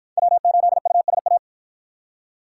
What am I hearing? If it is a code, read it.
Morse code, heard as a keyed tone, about 53 wpm, 700 Hz. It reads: W8USA